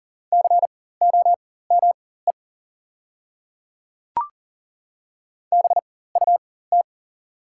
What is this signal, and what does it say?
Morse code, heard as a keyed tone, about 40 wpm, spaced as for 19 wpm, 690 Hz. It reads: COME BUT
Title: 4x Character Spacing - Top 100 Words - 40wpm